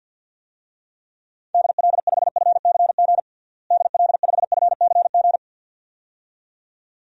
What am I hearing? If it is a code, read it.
Morse code, heard as a keyed tone, about 51 wpm, 700 Hz. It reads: DL5FCG DL5FCG
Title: DXCC-100 Call Signs - 6 Characters - Repeated 2x - 50wpm